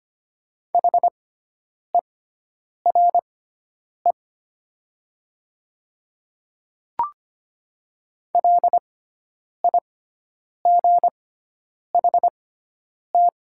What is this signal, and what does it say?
Morse code, heard as a keyed tone, about 25 wpm, spaced as for 9 wpm, 700 Hz. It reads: HERE LIGHT
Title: 6x Character Spacing - Top 300 Words - 25wpm